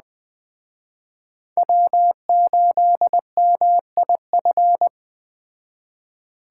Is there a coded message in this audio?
Yes: Morse code from 1.6 to 4.9 s, heard as a keyed tone, about 20 wpm, 700 Hz. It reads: W8MIF